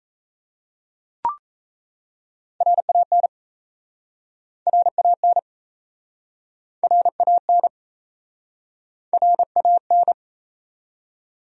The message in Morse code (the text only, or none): RAN RAN RAN RAN